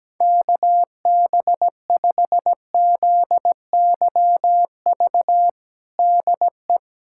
Morse code, heard as a keyed tone, about 17 wpm, 700 Hz. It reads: KB5ZYV DE
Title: USA Callsign DE Callsign - 6 Characters - 17wpm